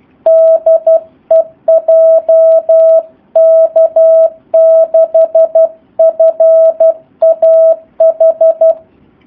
Morse code, heard as a keyed tone, about 12 wpm, 650 Hz. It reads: DE JK6FAH
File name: jh8cbh / 2019cw / SP04- w